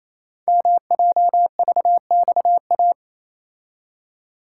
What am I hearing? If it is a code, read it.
Morse code, heard as a keyed tone, about 28 wpm, 700 Hz. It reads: MJVXA
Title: Code Groups - 5 Letters - 28wpm